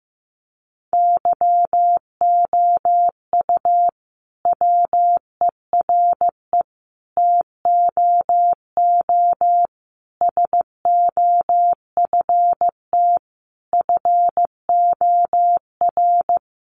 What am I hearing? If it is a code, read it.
Morse code, heard as a keyed tone, about 15 wpm, 700 Hz. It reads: YOU WERE TOO SOFT FOR
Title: Sentences from Top 1500 Words 15wpm